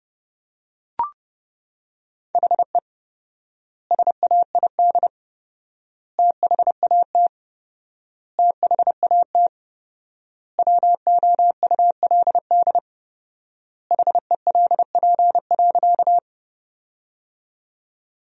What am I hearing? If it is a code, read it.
Morse code, heard as a keyed tone, about 32 wpm, 700 Hz. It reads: HE SAID THAT THAT WOULD HELP.
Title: 4x Word-Spacing - Sentences from Top 200 Words - 30wpm